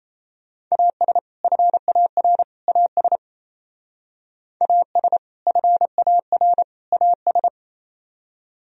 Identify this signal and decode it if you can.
Morse code, heard as a keyed tone, about 33 wpm, 700 Hz. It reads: AS FAR AS AS FAR AS
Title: Speed-Racing - Sets of 3 Words 28wpm